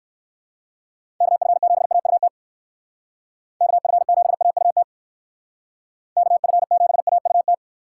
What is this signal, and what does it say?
Morse code, heard as a keyed tone, about 57 wpm, 700 Hz. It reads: KF7AUT KF7AUT KF7AUT